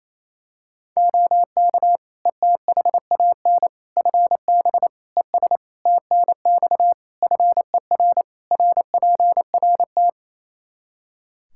Morse code, heard as a keyed tone, about 28 wpm, 700 Hz. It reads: OK ETHAN FB ES TNX FER RPRT
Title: Practice QSO - Exchange 2 - Info - 28wpm